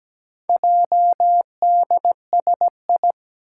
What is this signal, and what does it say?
Morse code, heard as a keyed tone, about 17 wpm, 700 Hz. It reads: JDSI
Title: Four Challenging Letter-Number Combinations 17wpm